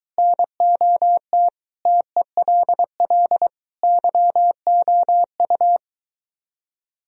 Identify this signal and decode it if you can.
Morse code, heard as a keyed tone, about 24 wpm, 700 Hz. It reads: NOT TELL YOU